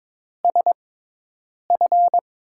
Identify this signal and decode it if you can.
Morse code, heard as a keyed tone, about 23 wpm, spaced as for 8 wpm, 700 Hz. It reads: SF
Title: USA Call Signs - 5 Characters - 6x Character Spacing - 22wpm